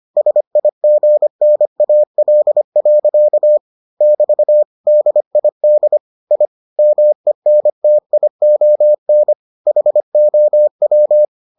Morse code, heard as a keyed tone, about 25 wpm, 600 Hz. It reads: SIGNAL. <BT> DID I MENTION HOW